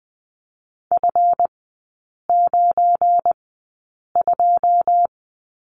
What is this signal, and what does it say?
Morse code, heard as a keyed tone, about 20 wpm, spaced as for 8 wpm, 700 Hz. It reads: F92
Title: 212 Lesson 34e - Number 8 - Three Characters 20wpm